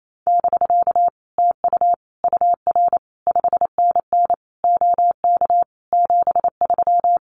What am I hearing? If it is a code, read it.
Morse code, heard as a keyed tone, about 28 wpm, 700 Hz. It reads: <BK> TU UR 5NN OK 73
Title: Common QSO Phrases Encore 28wpm